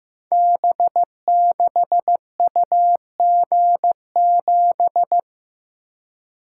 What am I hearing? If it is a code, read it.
Morse code, heard as a keyed tone, about 15 wpm, 700 Hz. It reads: B6UG7
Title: Code Groups - 5 Letter:Numbers - 15wpm